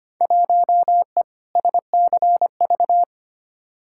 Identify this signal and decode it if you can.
Morse code, heard as a keyed tone, about 25 wpm, 700 Hz. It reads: 1E SCV
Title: Field Day Exchanges - 25wpm